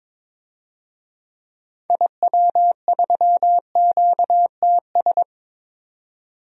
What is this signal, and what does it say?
Morse code, heard as a keyed tone, about 22 wpm, 700 Hz. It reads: IW3QTS
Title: DXCC-100 Call Signs - 6 Characters - 22wpm